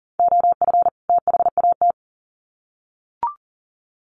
Morse code, heard as a keyed tone, about 40 wpm, 700 Hz. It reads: OF THAT
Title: Sets of 2 Words 40wpm